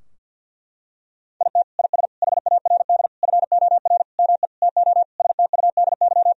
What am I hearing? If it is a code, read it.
Morse code, heard as a keyed tone, about 50 wpm, 700 Hz. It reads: IT IS HARD FOR ME TO STUDY
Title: Sentences from Top 1000 Words - Review Part 3 - 50wpm